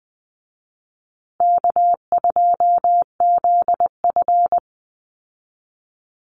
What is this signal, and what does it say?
Morse code, heard as a keyed tone, about 20 wpm, 700 Hz. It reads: K2ZF
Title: Easy Call Signs Repeated 3x - 20wpm